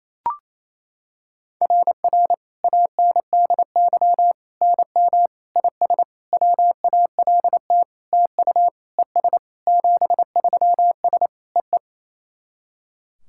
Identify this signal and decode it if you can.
Morse code, heard as a keyed tone, about 28 wpm, 690 Hz. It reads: RR ANDY NM IS WALT TU ES 73S EE
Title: SOTA QSO Practice 28wpm